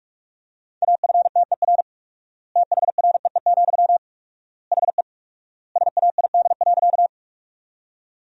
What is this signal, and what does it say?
Morse code, heard as a keyed tone, about 41 wpm, 700 Hz. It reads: AFTER THREE, HE SAID.
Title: 4x Word-Spacing - Sentences from Top 200 Words - 45wpm